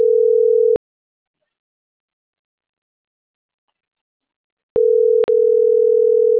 Ringback tone (North America), 440 + 480 Hz.
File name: CENTRAL SAGOT / out adriana.poncec - .82356